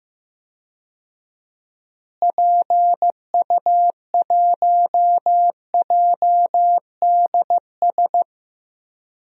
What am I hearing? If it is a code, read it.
Morse code, heard as a keyed tone, about 15 wpm, 700 Hz. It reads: PU1JDS
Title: DXCC-100 Call Signs - 6 Characters - Repeated 2x - 15wpm